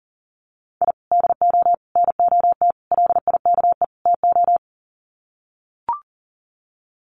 Morse code, heard as a keyed tone, about 40 wpm, 700 Hz. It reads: I DO NOT LIKE TO
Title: Sets of 5 Words 40wpm